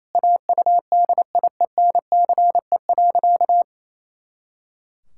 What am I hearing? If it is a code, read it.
Morse code, heard as a keyed tone, about 28 wpm, 700 Hz. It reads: AUDIENCE.
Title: Pangram Challenge v2 - 28wpm